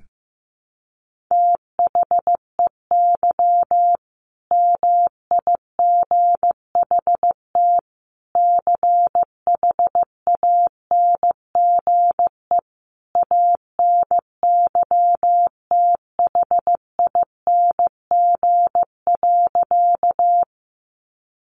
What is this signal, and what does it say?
Morse code, heard as a keyed tone, about 15 wpm, 700 Hz. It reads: THEY MIGHT CHANGE ANYTHING.
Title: A Sound Of Thunder 15wpm